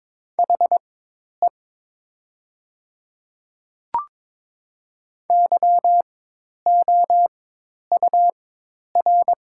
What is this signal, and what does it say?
Morse code, heard as a keyed tone, about 22 wpm, spaced as for 10 wpm, 700 Hz. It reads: HE YOUR
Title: 4x Character Spacing - Top 100 Words - 22wpm